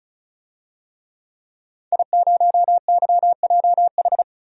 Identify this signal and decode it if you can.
Morse code, heard as a keyed tone, about 36 wpm, 700 Hz. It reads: I0YJH
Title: DXCC-100 Call Signs - 5 Characters - 35wpm